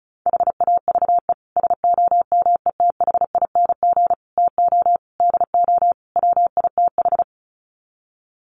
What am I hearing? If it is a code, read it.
Morse code, heard as a keyed tone, about 34 wpm, 700 Hz. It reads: HAVE SOMETHING TO DO WITH
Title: Sets of 5 Words Encore 35wpm